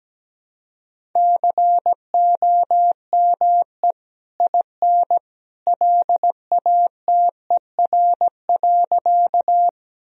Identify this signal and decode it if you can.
Morse code, heard as a keyed tone, about 17 wpm, 700 Hz. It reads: COME IN LATER.
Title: Sentences from Top 1000 Words - Review Part 4 - 17wpm